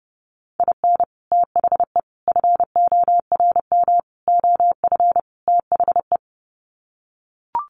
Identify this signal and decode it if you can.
Morse code, heard as a keyed tone, about 28 wpm, 700 Hz. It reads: IN THE FORM OF THE
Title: Sets of 5 Words Encore2 30wpm